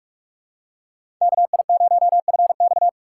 Morse code, heard as a keyed tone, about 44 wpm, 700 Hz. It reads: KI0FX